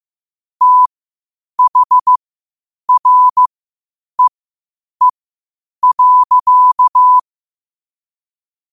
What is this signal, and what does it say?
Morse code, heard as a keyed tone, about 15 wpm, spaced as for 8 wpm, 1000 Hz. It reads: THREE.
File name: qotdfiles / 8wpm